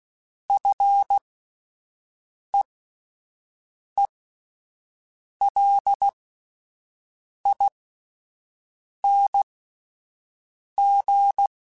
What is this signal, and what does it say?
Morse code, heard as a keyed tone, about 16 wpm, spaced as for 5 wpm, 800 Hz. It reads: FEELING
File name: Fs / Single F mp3s / Single F Prefixes 3x2 Twice @5